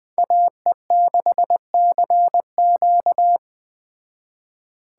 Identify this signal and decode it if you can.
Morse code, heard as a keyed tone, about 21 wpm, 700 Hz. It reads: AE6CQ
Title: Speed Racing - USA 2x2 Call Signs 17wpm